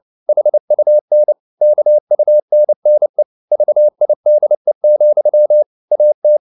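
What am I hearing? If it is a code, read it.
Morse code, heard as a keyed tone, about 29 wpm, 600 Hz. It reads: HUN KUNNE VIDE, AT
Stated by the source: Fast (QRQ) Morse at 29 wpm, from CW med Gnister.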